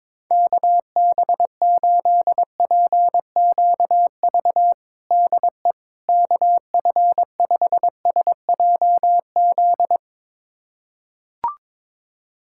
Morse code, heard as a keyed tone, about 22 wpm, 700 Hz. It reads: KB8PQV DE KF5SJZ
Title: USA Callsign DE Callsign - 6 Characters - 22wpm